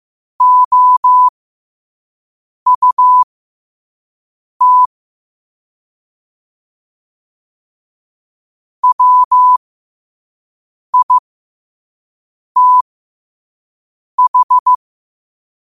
Morse code, heard as a keyed tone, about 15 wpm, spaced as for 5 wpm, 1000 Hz. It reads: OUT WITH
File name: qotdfiles / 5wpm